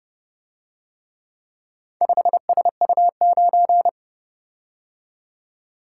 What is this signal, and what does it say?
Morse code, heard as a keyed tone, about 29 wpm, 700 Hz. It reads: 5SU9